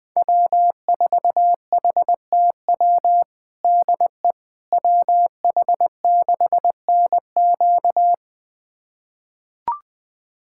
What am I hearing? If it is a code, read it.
Morse code, heard as a keyed tone, about 20 wpm, 700 Hz. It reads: W4HTW DE WH6NQ
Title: USA Callsign DE Callsign - 5 Characters - 20wpm